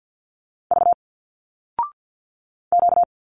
Morse code, heard as a keyed tone, about 50 wpm, 700 Hz. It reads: V Q
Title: Single Letters 50wpm